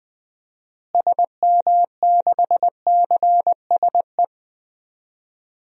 Morse code, heard as a keyed tone, about 20 wpm, 700 Hz. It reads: SM6CSE
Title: DXCC-100 Call Signs - 6 Characters - Repeated 2x - 20wpm